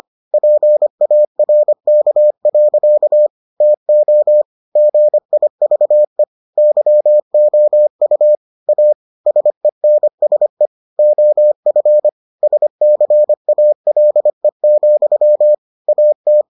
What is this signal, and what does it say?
Morse code, heard as a keyed tone, about 25 wpm, 600 Hz. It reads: PARK. TO GIVE YOU A SENSE OF SCALE, AT